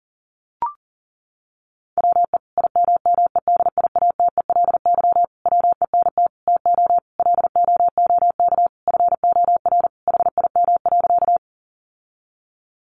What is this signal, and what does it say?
Morse code, heard as a keyed tone, about 40 wpm, 700 Hz. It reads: WE IMMEDIATELY WENT TO LOOK FOR HIM.
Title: Sentences from Top 1100 Words 40wpm